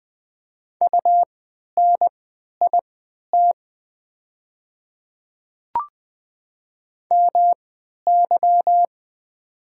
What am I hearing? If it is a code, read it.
Morse code, heard as a keyed tone, about 20 wpm, spaced as for 11 wpm, 700 Hz. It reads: UNIT MY
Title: 3x Character Spacing - Top 500 Words - 20wpm